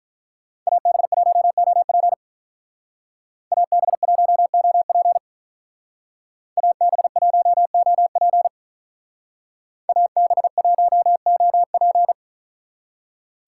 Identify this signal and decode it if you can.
Morse code, heard as a keyed tone, about 43 wpm, 700 Hz. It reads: AB1OP AB1OP AB1OP AB1OP
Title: Speed Racing - USA 2x2 Call Signs 35wpm